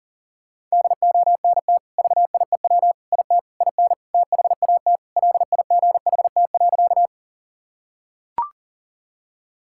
Morse code, heard as a keyed tone, about 37 wpm, 700 Hz. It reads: DONT VIEW IT IN THAT LIGHT.
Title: Sentences from Top 1000 Words - Review Part 6 - 40wpm